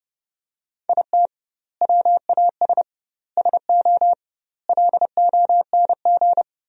Morse code, heard as a keyed tone, about 30 wpm, 700 Hz. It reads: IT WAS SO LONG
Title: 2x Word-Spacing - Sentences from Top 500 Words - 30wpm